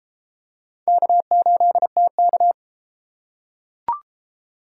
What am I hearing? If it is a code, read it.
Morse code, heard as a keyed tone, about 30 wpm, 700 Hz. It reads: K8TK